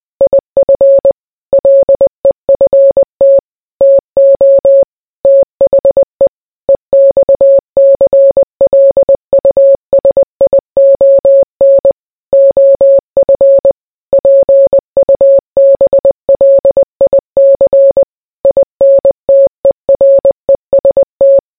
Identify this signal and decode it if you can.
Morse code, heard as a keyed tone, about 20 wpm, 550 Hz. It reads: IF LEFT TO THE EXCLUSION OF PUBLIC INTEREST